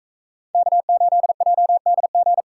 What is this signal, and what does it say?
Morse code, heard as a keyed tone, about 42 wpm, 700 Hz. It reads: K8JDG